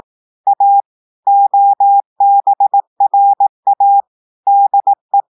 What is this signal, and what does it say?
Morse code, heard as a keyed tone, about 18 wpm, 800 Hz. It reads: A OBRA DE
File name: audios / portugais / portugais texte en clair 1 vitesse 18 mots minutes